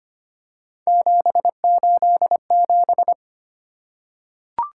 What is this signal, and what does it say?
Morse code, heard as a keyed tone, about 25 wpm, 700 Hz. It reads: 787
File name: Three Letter-Number Combinations - 25wpm